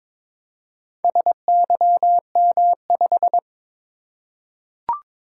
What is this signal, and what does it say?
Morse code, heard as a keyed tone, about 22 wpm, 700 Hz. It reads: SYM5
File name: practice / 22wpm / Four Letter-Number Combinations - 22wpm